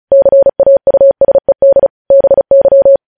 Morse code, heard as a keyed tone, about 36 wpm, 550 Hz. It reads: CAUSED BY
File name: podcast / News headlines-1x-35x35